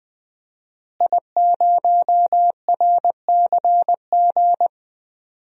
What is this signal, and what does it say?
Morse code, heard as a keyed tone, about 20 wpm, 700 Hz. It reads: I0RCG
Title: DXCC-100 Call Signs - 5 Characters - Repeated 2x - 20wpm